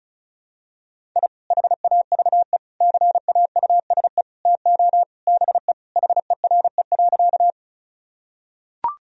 Morse code, heard as a keyed tone, about 36 wpm, 700 Hz. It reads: I HAVE CAUSE TO BE HERE.
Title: Sentences from Top 1000 Words - Review Part 1 - 35wpm